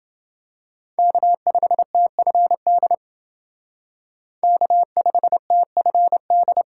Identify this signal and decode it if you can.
Morse code, heard as a keyed tone, about 28 wpm, 700 Hz. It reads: K5TFD K5TFD